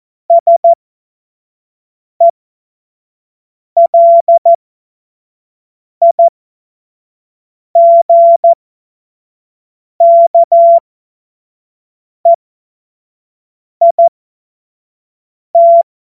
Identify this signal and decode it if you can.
Morse code, heard as a keyed tone, about 14 wpm, spaced as for 5 wpm, 680 Hz. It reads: SELIGKEIT
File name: Morsekurs / Witze / Witz 21. Dezember 2021, 50 F70